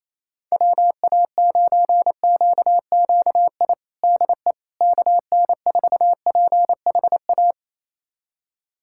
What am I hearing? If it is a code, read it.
Morse code, heard as a keyed tone, about 28 wpm, 700 Hz. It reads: WA9QQI DE KN4PHA
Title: USA Callsign DE Callsign 28wpm